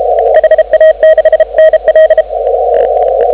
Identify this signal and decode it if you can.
Morse code, heard as a keyed tone, about 34 wpm, 1900 Hz, with noise. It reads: HA6NL
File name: CW / 2017 HA-OB HA5YG